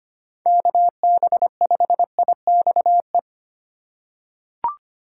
Morse code, heard as a keyed tone, about 25 wpm, 700 Hz. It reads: KB5IXE
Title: USA Call Signs - 4x Character Spacing - 25wpm